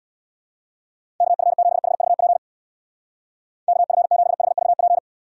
Morse code, heard as a keyed tone, about 66 wpm, 700 Hz. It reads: KF7HVP KF7HVP